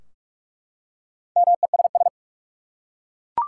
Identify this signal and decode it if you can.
Morse code, heard as a keyed tone, about 48 wpm, 700 Hz. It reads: MESS
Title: Four Letter Words - 45wpm